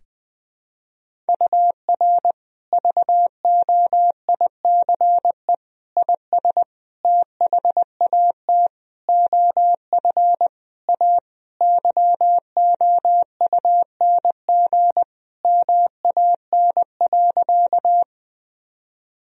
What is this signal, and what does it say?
Morse code, heard as a keyed tone, about 20 wpm, 700 Hz. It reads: UR VOICE IS THAT OF A YOUNG MAN.
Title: Sentences from Top 500 Words and Abbreviations 20wpm